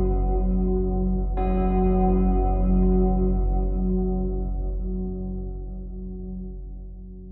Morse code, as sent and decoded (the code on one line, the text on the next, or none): none